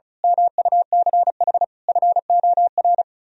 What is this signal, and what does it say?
Morse code, heard as a keyed tone, about 33 wpm, 700 Hz. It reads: MUCH FOR
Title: Speed-Racing - Sets of 5 Words 35wpm